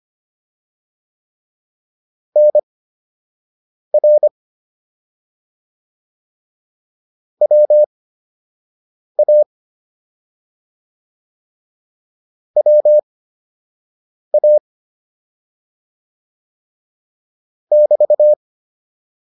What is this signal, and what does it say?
Morse code, heard as a keyed tone, about 25 wpm, spaced as for 6 wpm, 600 Hz. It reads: NR WA WA <BT>
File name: QSO 203 06